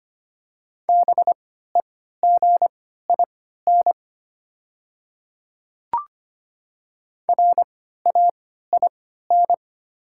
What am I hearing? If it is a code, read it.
Morse code, heard as a keyed tone, about 25 wpm, spaced as for 14 wpm, 700 Hz. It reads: BEGIN RAIN
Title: 3x Character Spacing - Top 300 Words - 25wpm